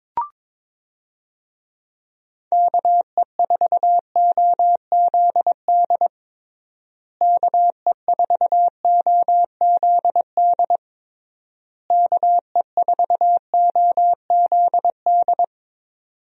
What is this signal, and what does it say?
Morse code, heard as a keyed tone, about 21 wpm, 700 Hz. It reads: KE4OZD KE4OZD KE4OZD
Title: USA Call Signs - 6 Characters - Repeated 3x - 22wpm